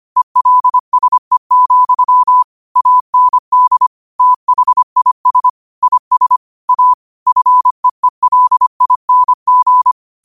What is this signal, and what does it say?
Morse code, heard as a keyed tone, about 25 wpm, 1000 Hz. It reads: ELSE, AND THIS IS A FEELING